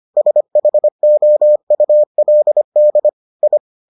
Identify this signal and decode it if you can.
Morse code, heard as a keyed tone, about 25 wpm, 600 Hz. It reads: SHOULD I